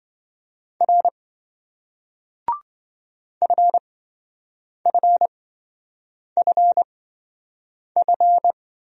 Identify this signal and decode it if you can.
Morse code, heard as a keyed tone, about 26 wpm, spaced as for 7 wpm, 700 Hz. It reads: R FFFF